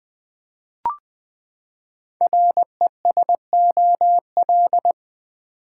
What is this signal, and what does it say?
Morse code, heard as a keyed tone, about 20 wpm, 700 Hz. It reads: RESOL
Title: Build A Word - 5 Letters At A Time - 10-Letter Words - 20wpm